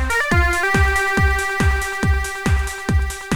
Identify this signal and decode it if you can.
There is no code.